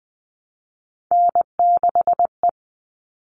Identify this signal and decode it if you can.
Morse code, heard as a keyed tone, about 20 wpm, 700 Hz. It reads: N6E
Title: All Cut Numbers - 3-Digit 20wpm